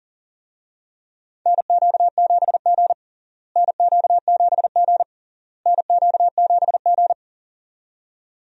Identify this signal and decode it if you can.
Morse code, heard as a keyed tone, about 40 wpm, 700 Hz. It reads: NQ7G NQ7G NQ7G
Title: USA Call Signs - 4 Characters - Repeated 3x - 40wpm